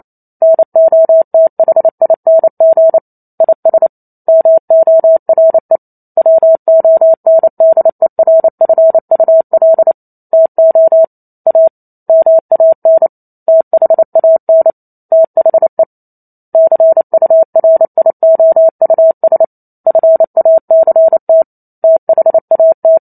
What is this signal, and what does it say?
Morse code, heard as a keyed tone, about 28 wpm, 650 Hz. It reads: NOTHING IS MORE WONDERFUL TO A MAN THAN THE CURIOUS FACT THAT
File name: Chapter 2 - 28 WPM